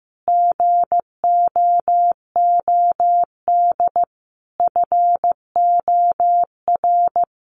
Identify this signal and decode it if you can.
Morse code, heard as a keyed tone, about 15 wpm, 700 Hz. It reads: GOOD FOR
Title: Sentences from Top 1100 Words 15wpm